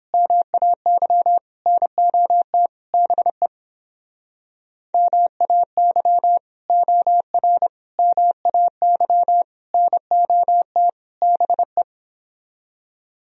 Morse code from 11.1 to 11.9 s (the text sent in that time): BE